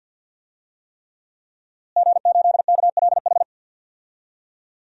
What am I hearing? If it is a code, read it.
Morse code, heard as a keyed tone, about 50 wpm, 700 Hz. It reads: G8KLH
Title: DXCC-100 Call Signs - 5 Characters - Repeated 3x - 50wpm